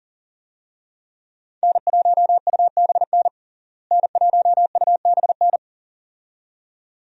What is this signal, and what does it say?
Morse code, heard as a keyed tone, about 40 wpm, 700 Hz. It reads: N1UBN N1UBN
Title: USA Call Signs - 5 Characters - Repeated 2x - 40wpm